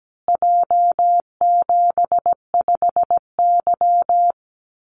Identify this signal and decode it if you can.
Morse code, heard as a keyed tone, about 17 wpm, 700 Hz. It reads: J75Y
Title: Call Signs Repeated 3x - 17wpm